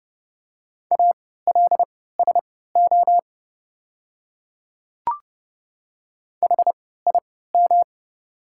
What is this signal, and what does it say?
Morse code, heard as a keyed tone, about 30 wpm, spaced as for 17 wpm, 700 Hz. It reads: ALSO HIM
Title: 3x Character Spacing - Top 200 Words - 30wpm